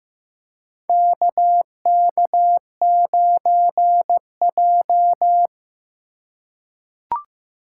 Morse code, heard as a keyed tone, about 15 wpm, 700 Hz. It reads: KK9J
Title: USA Call Signs - 4 Characters - 4x Character Spacing - 15wpm